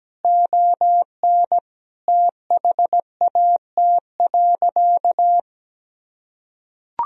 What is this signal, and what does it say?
Morse code, heard as a keyed tone, about 17 wpm, 700 Hz. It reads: ON THAT.
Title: Sentences from Top 1000 Words - Review Part 1 - 17wpm